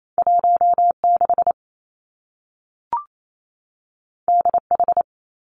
Morse code, heard as a keyed tone, about 28 wpm, 700 Hz. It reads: 1 6 D H
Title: Two Challenging Letter-Number Combinations 28wpm